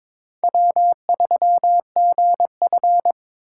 Morse code, heard as a keyed tone, about 22 wpm, 700 Hz. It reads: W3GF
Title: USA Call Signs - 3x Character Spacing - 22wpm